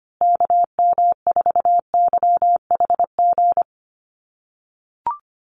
Morse code, heard as a keyed tone, about 25 wpm, 700 Hz. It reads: KM4YHG
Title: USA Call Signs Repeated 2x 25wpm